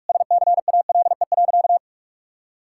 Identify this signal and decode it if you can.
Morse code, heard as a keyed tone, about 42 wpm, 700 Hz. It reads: SCALE.
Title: Sentences from Top 1000 Words - Review Part 5 - 45wpm